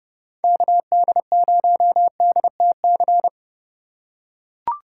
Morse code, heard as a keyed tone, about 30 wpm, 700 Hz. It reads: KD0DTC
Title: USA Call Signs - 6 Characters - 2x Character Spacing - 30wpm